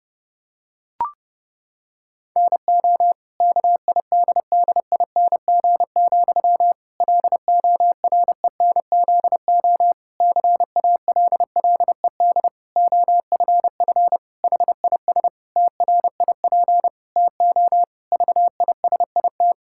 Morse code, heard as a keyed tone, about 30 wpm, 700 Hz. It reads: NO KIDDING, LORENZO CALLED OFF HIS TRIP TO VISIT
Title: Pangram Challenge v2 - 30wpm